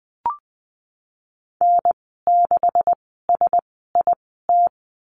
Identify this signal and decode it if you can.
Morse code, heard as a keyed tone, about 19 wpm, spaced as for 14 wpm, 700 Hz. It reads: N6SIT
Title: 207 Lesson 33g - Letter j - Call Signs 20wpm